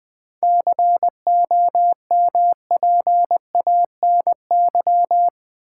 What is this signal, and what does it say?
Morse code, heard as a keyed tone, about 20 wpm, 700 Hz. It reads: COMPANY
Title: Sets of 2 Words - New Word Drill - 20wpm